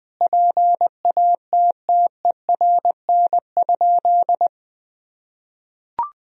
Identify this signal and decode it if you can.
Morse code, heard as a keyed tone, about 21 wpm, 700 Hz. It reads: PATTERN?
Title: Sentences from Top 1000 Words - Review Part 2 - 20wpm